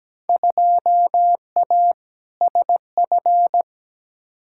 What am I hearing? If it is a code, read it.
Morse code, heard as a keyed tone, about 17 wpm, 700 Hz. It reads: 2A SF
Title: Field Day Exchanges - 17wpm